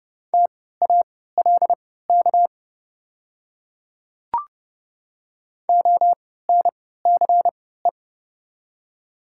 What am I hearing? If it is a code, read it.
Morse code, heard as a keyed tone, about 30 wpm, spaced as for 17 wpm, 700 Hz. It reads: TALK ONCE
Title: 3x Character Spacing - Top 500 Words - 30wpm